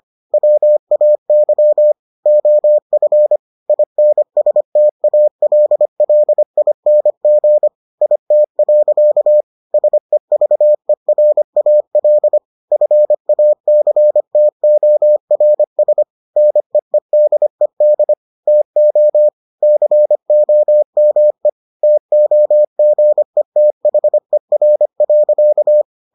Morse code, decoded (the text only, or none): WAY OF INSTALLING IT. SEVERAL FACTORS NEEDED TO COME TOGETHER.